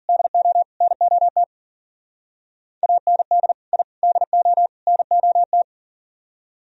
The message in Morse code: DO NOT AND I DO NOT

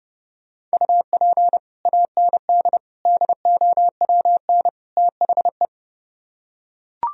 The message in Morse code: UP AND DOWN THE

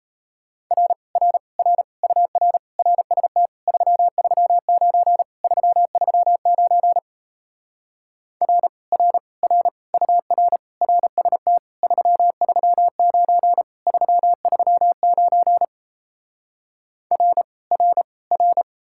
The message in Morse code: R R R UR RST 339 339 R R R UR RST 339 339 R R R

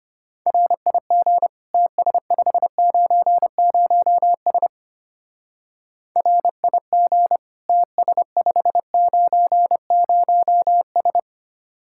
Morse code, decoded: RIG TS590S RIG TS590S

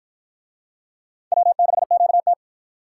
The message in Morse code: W6QT

